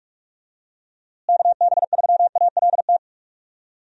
Morse code, heard as a keyed tone, about 46 wpm, 700 Hz. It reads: KB3ALT
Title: USA Call Signs - 6 Characters - Repeated 3x - 45wpm